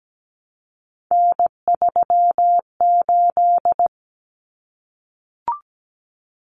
Morse code, heard as a keyed tone, about 17 wpm, 700 Hz. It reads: N38
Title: Common Cut Numbers - 3-Digit 17wpm